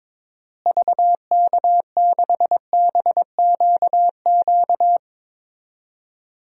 Morse code, heard as a keyed tone, about 22 wpm, 700 Hz. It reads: VK6BQQ
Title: DXCC-100 Call Signs - 6 Characters - Repeated 2x - 22wpm